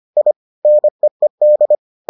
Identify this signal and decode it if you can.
Morse code, heard as a keyed tone, about 25 wpm, 600 Hz. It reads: I NEED